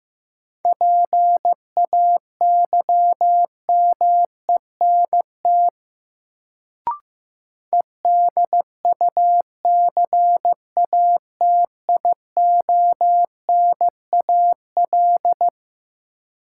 Morse code, heard as a keyed tone, about 15 wpm, 700 Hz. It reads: PAYMENT EDUCATIONAL
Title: Sentences from Top 1200 Words - New Word Drill Alt - 15wpm